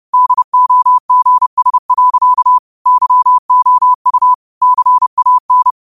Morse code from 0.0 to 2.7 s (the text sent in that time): DOGS.